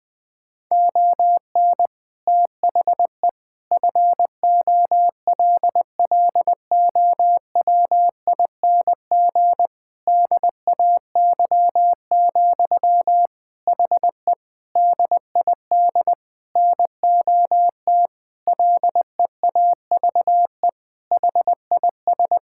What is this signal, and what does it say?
Morse code, heard as a keyed tone, about 20 wpm, 700 Hz. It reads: ON THE FOLLOWING DAY, HE DID NOT LEAVE HIS
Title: Sentences from Top 1000 Words - Review Part 6 - 20wpm